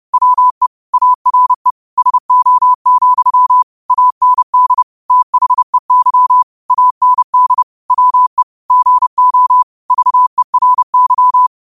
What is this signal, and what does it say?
Morse code, heard as a keyed tone, about 30 wpm, 1000 Hz. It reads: WE ARE SO, AND THEY AND WE GO VERY
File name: qotdfiles / 30wpm